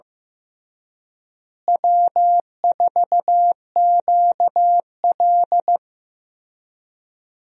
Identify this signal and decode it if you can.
Morse code, heard as a keyed tone, about 15 wpm, 700 Hz. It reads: W4QL
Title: USA Call Signs - 4 Characters - Repeated 3x - 15wpm